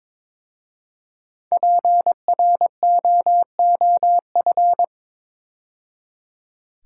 Morse code, heard as a keyed tone, about 22 wpm, 690 Hz. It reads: PROOF
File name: 5 Letter Words - First 3 Letters Given - 22wpm